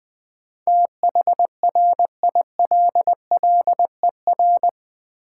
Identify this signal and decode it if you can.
Morse code, heard as a keyed tone, about 20 wpm, 690 Hz. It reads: THRILLER
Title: Difficult Words with Hints - 20wpm